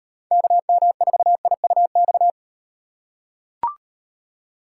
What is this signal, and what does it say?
Morse code, heard as a keyed tone, about 38 wpm, 700 Hz. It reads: KM4IUX